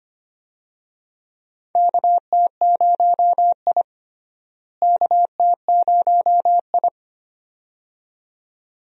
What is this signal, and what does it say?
Morse code, heard as a keyed tone, about 25 wpm, 700 Hz. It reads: KT0I KT0I
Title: USA Call Signs - 4 Characters - Repeated 2x - 25wpm